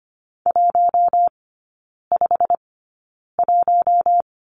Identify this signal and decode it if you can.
Morse code, heard as a keyed tone, about 25 wpm, spaced as for 9 wpm, 700 Hz. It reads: 151
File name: course / 25:10wpm / 093 Lesson 16e - Letter c - Three Characters 25wpm